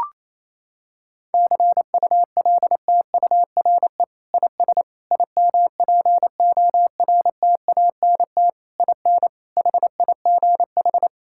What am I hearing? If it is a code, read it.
Morse code, heard as a keyed tone, about 28 wpm, 700 Hz. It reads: CULTURE IS IMPORTANT IN HIGH
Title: Sentences from Top 1000 Words - Review Part 6 - 28wpm